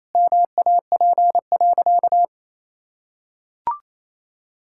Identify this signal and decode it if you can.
Morse code, heard as a keyed tone, about 28 wpm, 700 Hz. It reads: MAP.
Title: Sentences from Top 1000 Words - Review Part 6 - 28wpm